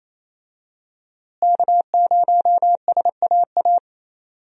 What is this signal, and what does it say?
Morse code, heard as a keyed tone, about 27 wpm, 700 Hz. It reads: K0SAA
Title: USA Call Signs - 5 Characters - Repeated 3x - 28wpm